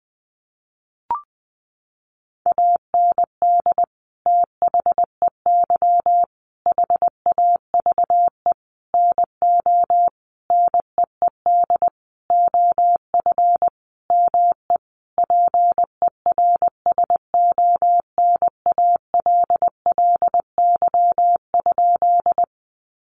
Morse code, heard as a keyed tone, about 20 wpm, 700 Hz. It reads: AND THEY HAVE NO NEED OF ME PERSONALLY?
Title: Sentences from Top 1400 Words 20wpm